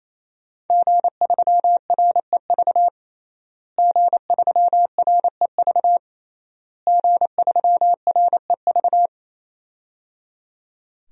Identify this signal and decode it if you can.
Morse code, heard as a keyed tone, about 28 wpm, 700 Hz. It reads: G3REV G3REV G3REV
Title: DXCC-100 Call Signs - 5 Characters - Repeated 3x - 28wpm